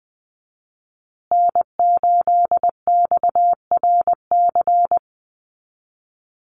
Morse code, heard as a keyed tone, about 20 wpm, 700 Hz. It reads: N8XRC
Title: USA Call Signs Repeated 3x 20wpm